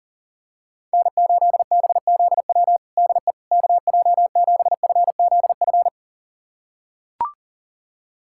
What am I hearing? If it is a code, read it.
Morse code, heard as a keyed tone, about 40 wpm, 700 Hz. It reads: N8BZW DE KJ7FZF
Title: USA Callsign DE Callsign 40wpm